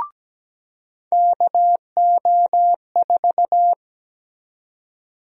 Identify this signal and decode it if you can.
Morse code, heard as a keyed tone, about 17 wpm, 700 Hz. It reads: KO4
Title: USA Call Sign Prefixes - 17wpm